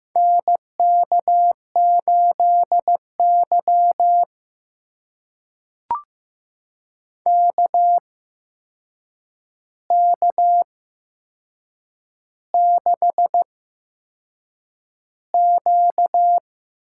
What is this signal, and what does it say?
Morse code, heard as a keyed tone, about 15 wpm, spaced as for 6 wpm, 700 Hz. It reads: NK8Y KK6Q
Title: USA Call Signs - 4 Characters - 8x Character Spacing - 15wpm